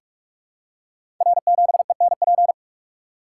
Morse code, heard as a keyed tone, about 44 wpm, 700 Hz. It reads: R7ENP
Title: Code Groups - 5 Letter:Numbers - 45wpm